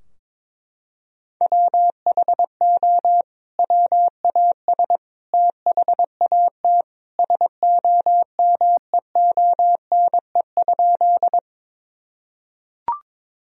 Morse code, heard as a keyed tone, about 22 wpm, 700 Hz. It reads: WHO WAS THAT SOMEONE?
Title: Sentences from Top 1000 Words - Review Part 4 - 22wpm